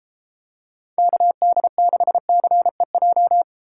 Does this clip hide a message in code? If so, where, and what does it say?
Yes: Morse code from 1.0 to 3.4 s, heard as a keyed tone, about 32 wpm, 700 Hz. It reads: KD6CEJ